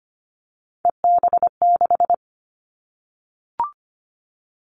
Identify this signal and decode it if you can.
Morse code, heard as a keyed tone, about 25 wpm, 700 Hz. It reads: EB6
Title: Callsign Prefixes for Top 50 DXCC - 25wpm